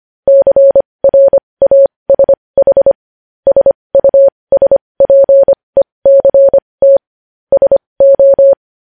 Morse code, heard as a keyed tone, about 25 wpm, spaced as for 20 wpm, 550 Hz. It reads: CRASH SUSPECT SO
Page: News Headlines in Morse Code at 20 WPM